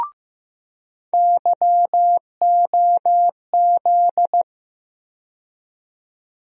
Morse code, heard as a keyed tone, about 15 wpm, 700 Hz. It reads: YOZ
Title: USA Call Sign Postfixes - 3 Letters 15wpm